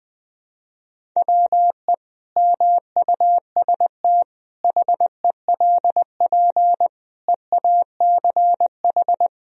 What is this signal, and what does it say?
Morse code, heard as a keyed tone, about 20 wpm, 700 Hz. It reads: WE MUST HELP EACH
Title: Sentences from Top 1000 Words - Review Part 1 - 20wpm